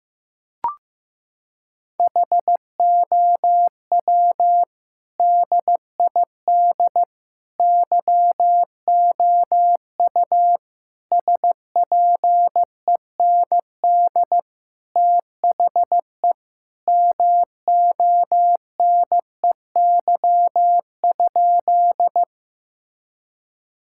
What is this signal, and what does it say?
Morse code, heard as a keyed tone, about 15 wpm, 700 Hz. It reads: HOW DID YOU SPEND THE MONEY?
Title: Sentences from Top 1000 Words - Review Part 5 - 15wpm